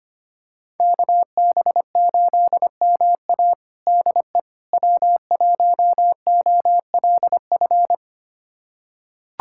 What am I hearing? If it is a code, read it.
Morse code, heard as a keyed tone, about 25 wpm, 700 Hz. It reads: KB8MA DE W1OLF
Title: USA Callsign DE Callsign - 5 Characters - 25wpm